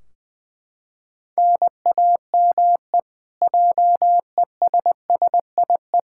Morse code, heard as a keyed tone, about 21 wpm, 700 Hz. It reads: NAME JESSIE
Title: Practice QSO - Exchange 1 - Beginning - 20wpm